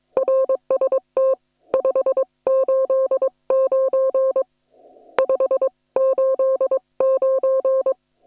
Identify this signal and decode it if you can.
Morse code, heard as a keyed tone, about 22 wpm, 530 Hz. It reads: RST 589 589